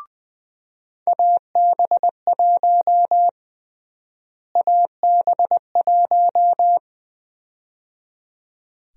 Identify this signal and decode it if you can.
Morse code, heard as a keyed tone, about 20 wpm, 700 Hz. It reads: AB1 AB1
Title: USA Call Sign Prefixes - Repeated 2x - 20wpm